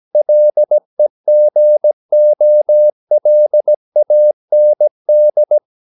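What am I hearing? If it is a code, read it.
Morse code, heard as a keyed tone, about 17 wpm, 600 Hz. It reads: LEGOLAND
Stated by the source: Slow (QRS) Morse practice at 17 wpm, from CW med Gnister.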